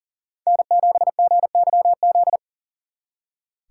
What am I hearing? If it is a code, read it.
Morse code, heard as a keyed tone, about 50 wpm, 700 Hz. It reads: N7GYZ